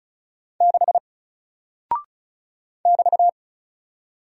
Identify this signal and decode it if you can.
Morse code, heard as a keyed tone, about 34 wpm, 700 Hz. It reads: 6 <BT>
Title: Single Character (Letter:Number:Punctuation:Prosign) Alt 35wpm